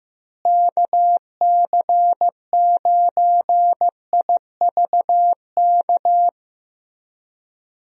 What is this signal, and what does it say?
Morse code, heard as a keyed tone, about 15 wpm, 700 Hz. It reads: KC9IVK
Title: USA Call Signs - 6 Characters - Repeated 3x - 15wpm